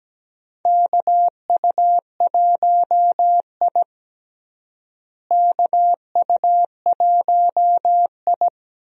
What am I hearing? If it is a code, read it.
Morse code, heard as a keyed tone, about 17 wpm, 700 Hz. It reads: KU1I KU1I
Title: USA Call Signs - 4 Characters - Repeated 2x - 17wpm